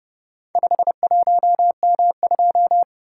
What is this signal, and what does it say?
Morse code, heard as a keyed tone, about 29 wpm, 700 Hz. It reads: 51M2